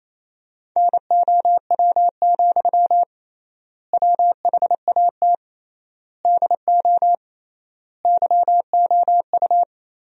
Morse code, heard as a keyed tone, about 28 wpm, 700 Hz. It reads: NOW, WHAT DO YOU
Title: 3x Word-Spacing - Sentences from Top 200 Words - 28wpm